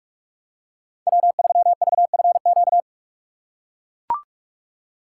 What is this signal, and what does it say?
Morse code, heard as a keyed tone, about 44 wpm, 700 Hz. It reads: W3VFQ
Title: USA Call Signs - 6x Character Spacing - 45wpm